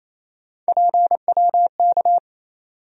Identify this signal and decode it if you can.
Morse code, heard as a keyed tone, about 28 wpm, 700 Hz. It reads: PWK
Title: USA Call Sign Postfixes - 3 Letters 28wpm